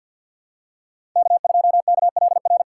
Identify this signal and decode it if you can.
Morse code, heard as a keyed tone, about 53 wpm, 700 Hz. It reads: K2KLR
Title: USA Call Signs - 5 Characters - 3x Character Spacing - 50wpm